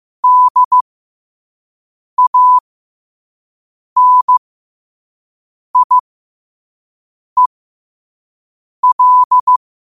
Morse code, heard as a keyed tone, about 15 wpm, spaced as for 5 wpm, 1000 Hz. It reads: DANIEL
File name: qotdfiles / 5wpm